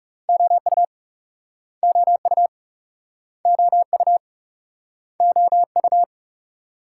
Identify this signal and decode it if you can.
Morse code, heard as a keyed tone, about 36 wpm, 700 Hz. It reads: OU OU OU OU
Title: Speed-Racing Multi-letter Phonemes - 30wpm